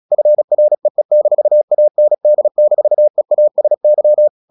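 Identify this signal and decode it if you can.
Morse code, heard as a keyed tone, about 36 wpm, 600 Hz. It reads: FREE-AND-EASY
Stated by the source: Morse practice at 36 wpm, text of Ken Follett's novel Never, from CW med Gnister.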